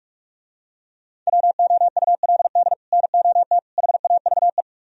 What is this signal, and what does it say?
Morse code, heard as a keyed tone, about 47 wpm, 700 Hz. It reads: WOULD NOT HAVE